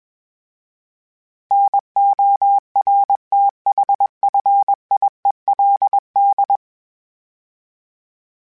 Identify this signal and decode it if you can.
Morse code, heard as a keyed tone, about 21 wpm, 800 Hz. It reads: NORTHFIELD
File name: fromSMs / FromSMs mp3s / Cities@20